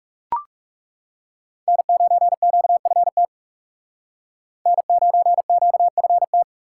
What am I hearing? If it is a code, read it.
Morse code, heard as a keyed tone, about 39 wpm, 700 Hz. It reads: N9QFT N9QFT